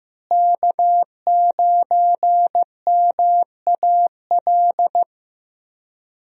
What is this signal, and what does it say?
Morse code, heard as a keyed tone, about 15 wpm, 700 Hz. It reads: K9MAL
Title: USA Call Signs - 5 Characters - Repeated 3x - 15wpm